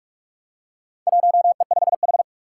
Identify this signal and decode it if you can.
Morse code, heard as a keyed tone, about 44 wpm, 700 Hz. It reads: 1E5H
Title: Four Challenging Letter-Number Combinations 45wpm